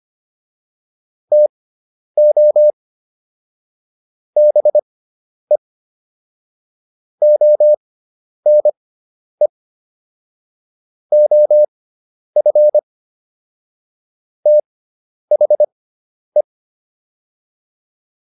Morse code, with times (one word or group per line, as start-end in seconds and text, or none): 1.3-2.7 TO
4.4-5.6 BE
7.2-9.5 ONE
11.1-12.8 OF
14.5-16.4 THE